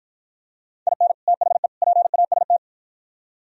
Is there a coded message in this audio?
Yes: Morse code from 0.9 to 2.6 s, heard as a keyed tone, about 53 wpm, 700 Hz. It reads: IN THE PAST